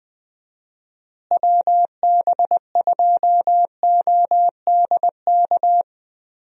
Morse code, heard as a keyed tone, about 20 wpm, 700 Hz. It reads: WB2ODK
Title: USA Call Signs - 6 Characters - Repeated 2x - 20wpm